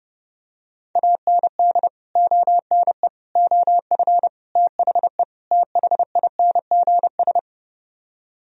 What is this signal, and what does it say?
Morse code, heard as a keyed tone, about 30 wpm, 700 Hz. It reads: AND ONE OF THE THINGS